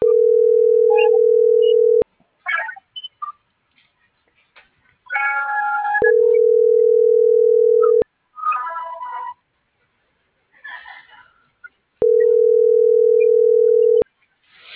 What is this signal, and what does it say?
Ringback tone (North America), 440 + 480 Hz.